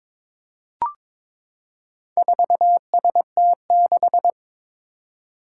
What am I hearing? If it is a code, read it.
Morse code, heard as a keyed tone, about 22 wpm, 700 Hz. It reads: 4ST6
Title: Four Challenging Letter-Number Combinations 22wpm